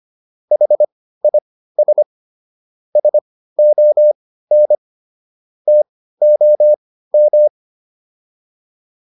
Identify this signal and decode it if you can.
Morse code, heard as a keyed tone, about 25 wpm, spaced as for 15 wpm, 600 Hz. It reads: HIS SON TOM